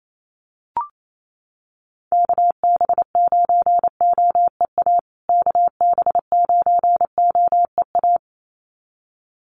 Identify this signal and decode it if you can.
Morse code, heard as a keyed tone, about 28 wpm, 700 Hz. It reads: KB9OEA KB9OEA
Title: USA Call Signs Repeated 2x 28wpm